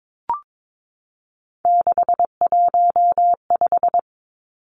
Morse code, heard as a keyed tone, about 22 wpm, 700 Hz. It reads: 615
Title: Three Challenging Letter-Number Combinations 22wpm